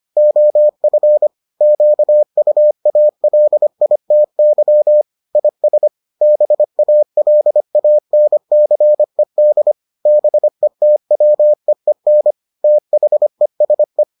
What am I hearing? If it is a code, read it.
Morse code, heard as a keyed tone, about 25 wpm, 600 Hz. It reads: OF QUALITY IS BALANCED BETWEEN THESE